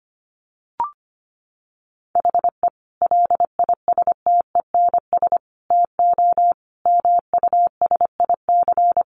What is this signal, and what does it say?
Morse code, heard as a keyed tone, about 25 wpm, 700 Hz. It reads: HE LISTENS TO MUSIC
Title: Additional Easy Phrases 25wpm